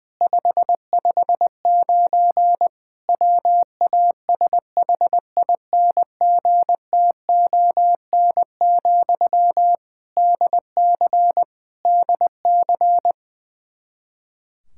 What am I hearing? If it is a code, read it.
Morse code, heard as a keyed tone, about 20 wpm, 700 Hz. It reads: 559 WASHINGTON, DC DC
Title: Practice QSO - Exchange 1 - Beginning - 20wpm